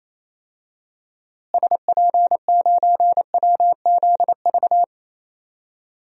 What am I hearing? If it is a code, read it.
Morse code, heard as a keyed tone, about 28 wpm, 700 Hz. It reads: SP9WZV
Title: DXCC-100 Call Signs - 6 Characters - Repeated 2x - 28wpm